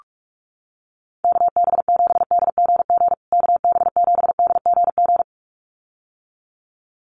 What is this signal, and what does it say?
Morse code, heard as a keyed tone, about 44 wpm, 700 Hz. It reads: KB7DGG KB7DGG
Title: USA Call Signs Repeated 2x 45wpm